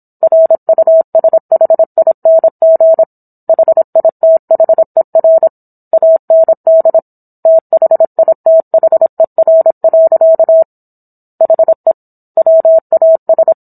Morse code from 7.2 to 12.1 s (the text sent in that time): THITHER. HE